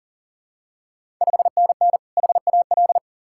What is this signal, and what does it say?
Morse code, heard as a keyed tone, about 40 wpm, 700 Hz. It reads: 5NN HAL
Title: Ontario QSO Party Exchange 40wpm